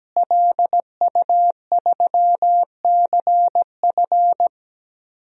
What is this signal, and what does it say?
Morse code, heard as a keyed tone, about 17 wpm, 700 Hz. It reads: LU3CF
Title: DXCC-100 Call Signs - 5 Characters - Repeated 3x - 17wpm